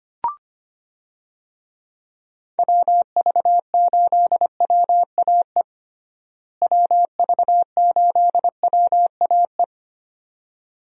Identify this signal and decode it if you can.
Morse code, heard as a keyed tone, about 25 wpm, 700 Hz. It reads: WV8WAE WV8WAE
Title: USA Call Signs - 6 Characters - Repeated 2x - 25wpm